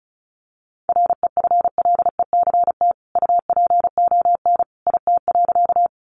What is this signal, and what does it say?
Morse code, heard as a keyed tone, about 34 wpm, 700 Hz. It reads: REFLECT UPON IT.
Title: Sentences from Top 1500 Words 35wpm